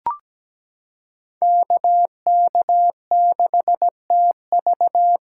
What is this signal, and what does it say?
Morse code, heard as a keyed tone, about 17 wpm, 700 Hz. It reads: KK6TV
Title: K1USN SST QSO Practice - 17wpm